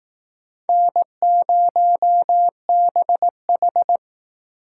Morse code, heard as a keyed tone, about 19 wpm, 700 Hz. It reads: N0BH
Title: Speed Racing - USA 1x2 Call Signs 15wpm